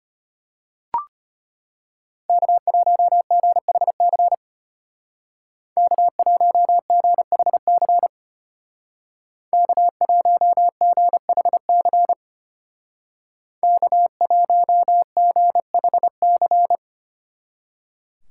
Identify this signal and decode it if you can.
Morse code, heard as a keyed tone, about 31 wpm, 700 Hz. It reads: K1GHC K1GHC K1GHC K1GHC
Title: Speed Racing - USA 1x3 Call Signs 25wpm